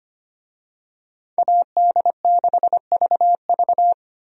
Morse code, heard as a keyed tone, about 25 wpm, 700 Hz. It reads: AD6VV
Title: USA Call Signs - 5 Characters - 25wpm